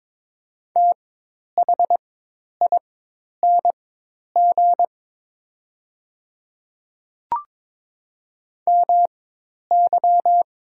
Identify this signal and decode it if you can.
Morse code, heard as a keyed tone, about 23 wpm, spaced as for 10 wpm, 700 Hz. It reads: THING MY
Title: 4x Character Spacing - Top 100 Words - 22wpm